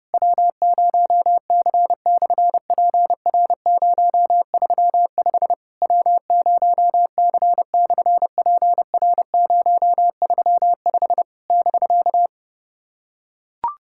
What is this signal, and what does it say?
Morse code, heard as a keyed tone, about 30 wpm, 700 Hz. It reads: W0C/PR035 W0C/PR035 <BK>
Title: SOTA QSO Practice 30wpm